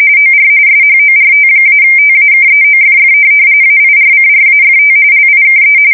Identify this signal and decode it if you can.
RTTY, about 45.45 baud. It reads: OR SHORT-TAKEOFF AIRCRAFT LIKE
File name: fsk170 / USS-New-Jersey